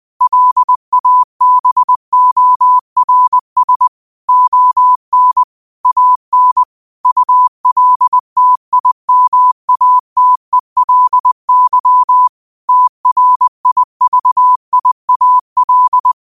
Morse code, heard as a keyed tone, about 20 wpm, 1000 Hz. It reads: LABORS ON AN ULTIMATELY TRIVIAL
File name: qotdfiles / 20wpm